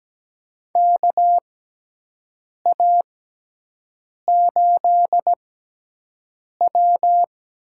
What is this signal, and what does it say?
Morse code, heard as a keyed tone, about 17 wpm, spaced as for 6 wpm, 700 Hz. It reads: KA8W
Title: USA Call Signs - 6x Character Spacing - 17wpm